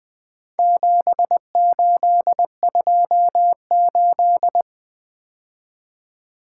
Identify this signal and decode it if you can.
Morse code, heard as a keyed tone, about 20 wpm, 700 Hz. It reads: 7828
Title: Four Letter-Number Combinations - 20wpm